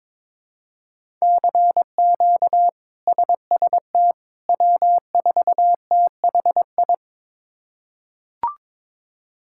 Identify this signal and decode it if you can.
Morse code, heard as a keyed tone, about 22 wpm, 700 Hz. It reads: CQ SST W4THI
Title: K1USN SST QSO Practice - 22wpm